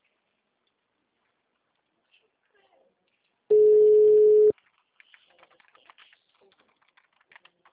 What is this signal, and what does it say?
Ringback tone (Europe, 425 Hz), with speech.
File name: out- -2019- - - .144